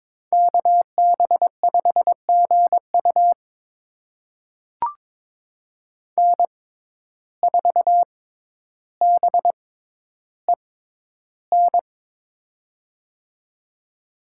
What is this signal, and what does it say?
Morse code, heard as a keyed tone, about 22 wpm, spaced as for 11 wpm, 700 Hz. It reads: KB5GU N4BEN
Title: USA Call Signs - 6x Character Spacing - 22wpm